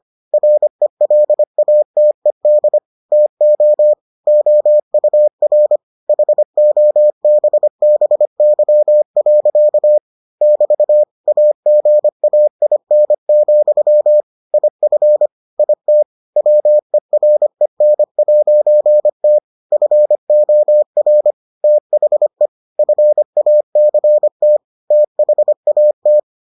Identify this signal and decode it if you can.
Morse code, heard as a keyed tone, about 25 wpm, 600 Hz. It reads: RELATED TO OUR HOBBY. <BT> AGAIN, IF IT WEREN'T FOR THE FACT THAT